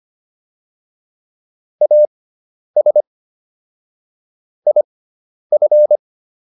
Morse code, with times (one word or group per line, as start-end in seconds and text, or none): 1.8-3.0 AS
4.7-6.0 IF